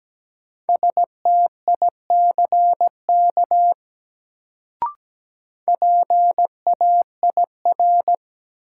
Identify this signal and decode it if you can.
Morse code, heard as a keyed tone, about 17 wpm, 700 Hz. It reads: STICK PAIR
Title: Sentences from Top 1200 Words - New Word Drill Alt - 17wpm